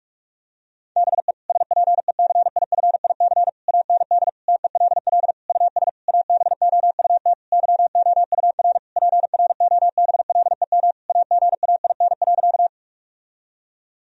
Morse code, heard as a keyed tone, about 46 wpm, 700 Hz. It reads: BE SPECIFIC AND TELL US ABOUT YOUR PROBLEM AGAIN.
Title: Sentences from Top 1000 Words - Review Part 3 - 45wpm